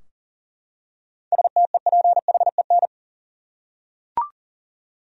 Morse code, heard as a keyed tone, about 40 wpm, 700 Hz. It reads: STEPHEN
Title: USA Common Names - 40wpm